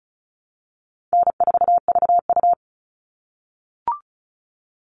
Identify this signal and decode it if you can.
Morse code, heard as a keyed tone, about 34 wpm, 700 Hz. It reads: N4VU
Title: All Cut Numbers - 4-Digit 35wpm